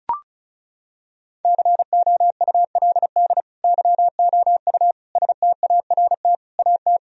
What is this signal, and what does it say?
Morse code, heard as a keyed tone, about 36 wpm, 700 Hz. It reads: COULD YOU START AT
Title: Sentences from Top 1000 Words - Review Part 1 - 35wpm